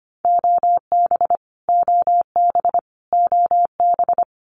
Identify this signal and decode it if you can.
Morse code, heard as a keyed tone, about 25 wpm, 700 Hz. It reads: OB OB OB
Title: Abbreviations Repeated 3x - 25wpm